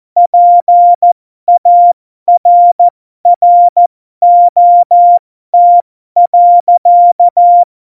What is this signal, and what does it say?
Morse code, heard as a keyed tone, about 14 wpm, spaced as for 12 wpm, 700 Hz. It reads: PARROT.
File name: Morsekurs / Witze